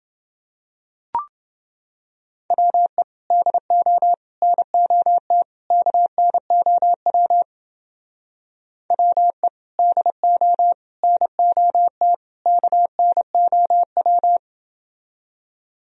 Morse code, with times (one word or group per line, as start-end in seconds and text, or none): 2.5-3.0 WE
3.3-4.1 DO
4.4-5.4 NOT
5.7-7.4 KNOW
8.9-9.5 WE
9.8-10.7 DO
11.0-12.1 NOT
12.5-14.4 KNOW